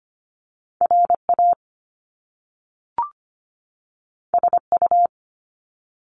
Morse code, heard as a keyed tone, about 25 wpm, 700 Hz. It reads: RA SU
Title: Top Two Letter Combinations in Words 25wpm